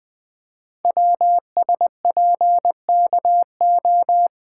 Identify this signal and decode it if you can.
Morse code, heard as a keyed tone, about 20 wpm, 700 Hz. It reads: WSPKO
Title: Code Groups - 5 Letters - 20wpm